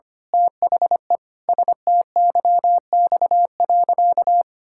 Morse code, heard as a keyed tone, about 25 wpm, 700 Hz. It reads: THE STYX.
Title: Pangram Challenge v2 - 25wpm